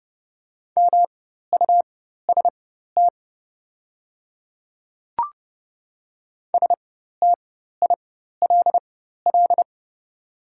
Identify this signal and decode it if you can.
Morse code, heard as a keyed tone, about 30 wpm, spaced as for 14 wpm, 700 Hz. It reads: MUST STILL
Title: 4x Character Spacing - Top 500 Words - 30wpm